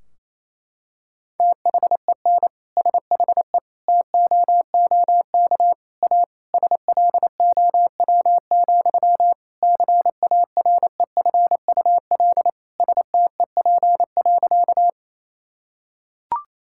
Morse code, heard as a keyed tone, about 28 wpm, 700 Hz. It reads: THEN SHE TOOK A SLOW, CAREFUL STEP.
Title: Sentences from Top 1600 Words 28wpm